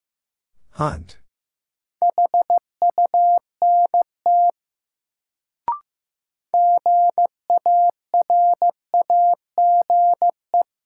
Morse code, heard as a keyed tone, about 15 wpm, 700 Hz. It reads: HUNT GARAGE
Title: Sentences from Top 1600 Words - New Word Drill - 15wpm